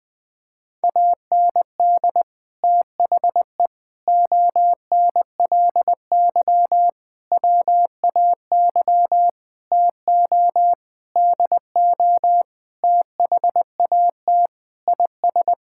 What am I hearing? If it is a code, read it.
Morse code, heard as a keyed tone, about 20 wpm, 700 Hz. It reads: AND THE ONLY WAY TO DO THAT IS
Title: Sets of 9 Words - 20wpm